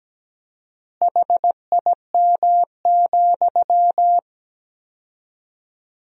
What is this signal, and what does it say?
Morse code, heard as a keyed tone, about 17 wpm, 700 Hz. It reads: HIM,
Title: 4x Word-Spacing - Sentences from Top 400 Words - 17wpm